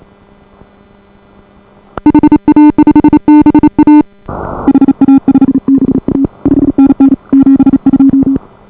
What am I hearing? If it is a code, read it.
Morse code, heard as a keyed tone, about 34 wpm, 280 Hz. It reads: HA5BA HA5BA 5NN 72